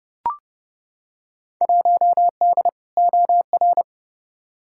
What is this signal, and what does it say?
Morse code, heard as a keyed tone, about 30 wpm, 700 Hz. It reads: 1D OR
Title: Field Day Exchanges - 30wpm